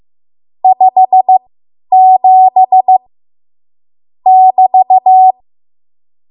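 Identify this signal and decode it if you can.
Morse code, heard as a keyed tone, about 15 wpm, spaced as for 10 wpm, 750 Hz. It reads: 57 <BT>
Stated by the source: Morse code practice run at 10 WPM from a code practice archive.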